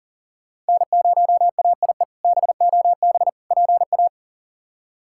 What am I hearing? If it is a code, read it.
Morse code, heard as a keyed tone, about 40 wpm, 700 Hz. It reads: N0AIE BOB PA
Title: K1USN SST QSO Practice - 40wpm